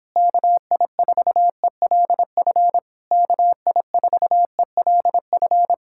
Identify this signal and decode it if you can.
Morse code, heard as a keyed tone, about 26 wpm, 700 Hz. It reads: KI4ELF KI4ELF